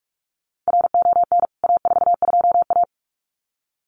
Morse code, heard as a keyed tone, about 44 wpm, 700 Hz. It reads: RON A42A
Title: CWT Exchange 45wpm